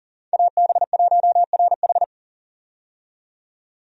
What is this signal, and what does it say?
Morse code, heard as a keyed tone, about 40 wpm, 700 Hz. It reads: AB1RH
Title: USA Call Signs - 5 Characters - 40wpm